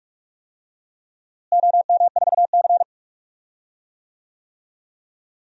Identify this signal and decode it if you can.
Morse code, heard as a keyed tone, about 45 wpm, 700 Hz. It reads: OM4C
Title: DXCC-100 Call Signs - 4 Characters - Repeated 3x - 45wpm